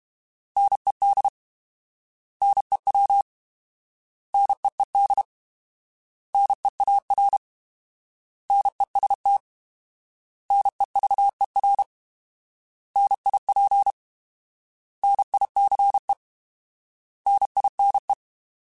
Morse code, heard as a keyed tone, about 32 wpm, 800 Hz. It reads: NED NEW NEED NEAR NEST NEVER NIP NICE NINE